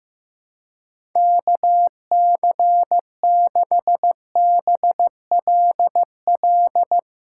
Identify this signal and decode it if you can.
Morse code, heard as a keyed tone, about 15 wpm, 700 Hz. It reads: KC6BLL
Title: Speed Racing - USA Call Signs 15wpm